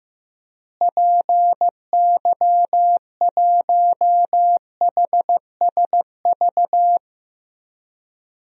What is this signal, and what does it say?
Morse code, heard as a keyed tone, about 15 wpm, 700 Hz. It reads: PY1HSV
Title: DXCC-100 Call Signs - 6 Characters - Repeated 2x - 15wpm